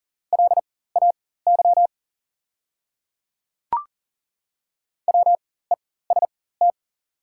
Morse code, heard as a keyed tone, about 40 wpm, spaced as for 19 wpm, 690 Hz. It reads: LAY WEST
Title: 4x Character Spacing - Top 500 Words - 40wpm